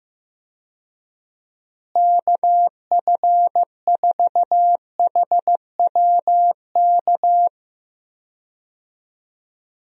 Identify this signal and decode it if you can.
Morse code, heard as a keyed tone, about 15 wpm, 700 Hz. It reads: KF4HWK
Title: USA Call Signs - 6 Characters - Repeated 2x - 15wpm